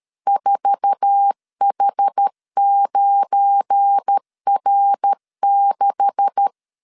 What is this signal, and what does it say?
Morse code, heard as a keyed tone, about 13 wpm, 800 Hz. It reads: 4H9R6
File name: CW / DET / 1 9 1500C S B 12wpm 3819